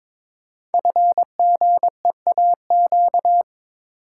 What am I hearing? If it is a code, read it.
Morse code, heard as a keyed tone, about 22 wpm, 690 Hz. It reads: FGEAQ
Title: Code Groups - 5 Letters - 22wpm